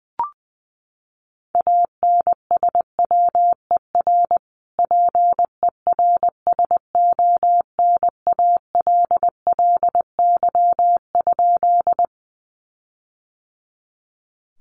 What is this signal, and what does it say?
Morse code, heard as a keyed tone, about 20 wpm, 700 Hz. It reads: ANSWER PERSONALLY?
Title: Sentences from Top 1400 Words 20wpm